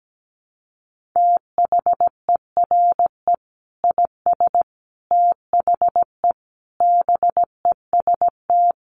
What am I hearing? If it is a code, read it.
Morse code, heard as a keyed tone, about 17 wpm, 700 Hz. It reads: THERE IS THE BEST
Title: Sentences from Top 1500 Words 17wpm